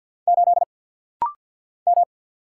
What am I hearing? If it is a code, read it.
Morse code, heard as a keyed tone, about 49 wpm, 700 Hz. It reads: 8 M
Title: Single Letter-Number Alt - Rapid-Fire 50wpm